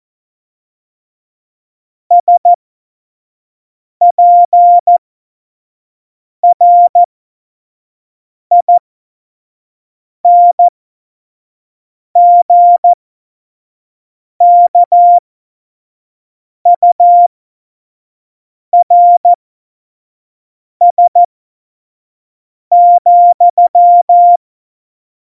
Morse code, heard as a keyed tone, about 14 wpm, spaced as for 5 wpm, 700 Hz. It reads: SPRINGKURS,
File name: Morsekurs / Witze / Witz 04 August 2020, 25 F70